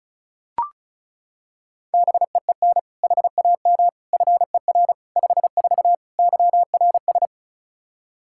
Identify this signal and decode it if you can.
Morse code, heard as a keyed tone, about 36 wpm, 700 Hz. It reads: BEEN HAM FER 54 YRS
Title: Practice QSO - Exchange 3 - More Info - 35wpm